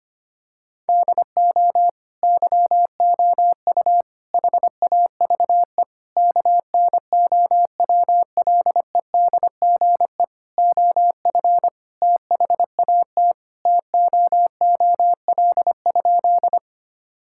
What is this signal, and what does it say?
Morse code, heard as a keyed tone, about 25 wpm, 700 Hz. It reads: DO YOU HAVE KNOWLEDGE OF THAT TOOL?
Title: Sentences from Top 1000 Words - Review Part 6 - 25wpm